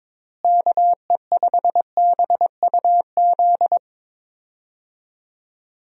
Morse code, heard as a keyed tone, about 22 wpm, 700 Hz. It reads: KE5BUZ
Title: USA Call Signs - 6 Characters - Repeated 2x - 22wpm